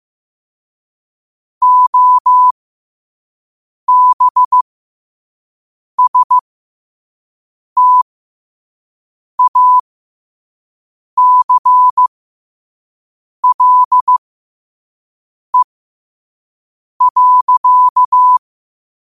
Morse code, heard as a keyed tone, about 15 wpm, spaced as for 5 wpm, 1000 Hz. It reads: OBSTACLE.